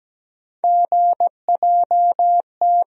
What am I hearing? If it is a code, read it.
Morse code, heard as a keyed tone, about 17 wpm, 700 Hz. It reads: GJT
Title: USA Call Sign Postfixes - 3 Letters 17wpm